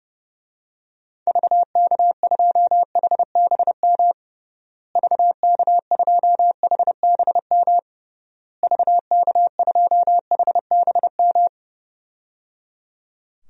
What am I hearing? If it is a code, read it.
Morse code, heard as a keyed tone, about 30 wpm, 700 Hz. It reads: VK2HBM VK2HBM VK2HBM
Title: DXCC-100 Call Signs - 6 Characters - Repeated 3x - 30wpm